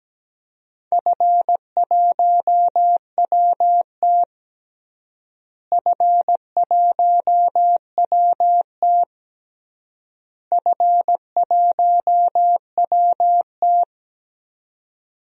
Morse code, heard as a keyed tone, about 17 wpm, 700 Hz. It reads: F1WT F1WT F1WT
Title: DXCC-100 Call Signs - 4 Characters - Repeated 3x - 17wpm